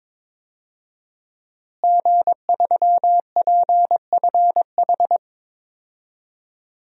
Morse code, heard as a keyed tone, about 22 wpm, 700 Hz. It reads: G3PFH
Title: DXCC-100 Call Signs - 5 Characters - Repeated 3x - 22wpm